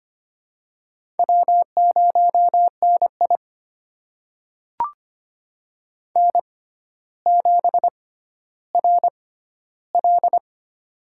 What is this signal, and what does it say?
Morse code, heard as a keyed tone, about 25 wpm, spaced as for 13 wpm, 700 Hz. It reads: W0NI N7RL
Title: USA Call Signs - 4 Characters - 6x Character Spacing - 25wpm